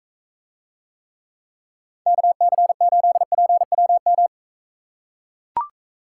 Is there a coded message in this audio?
Yes: Morse code from 2.1 to 4.3 s, heard as a keyed tone, about 42 wpm, 700 Hz. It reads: KC8PWM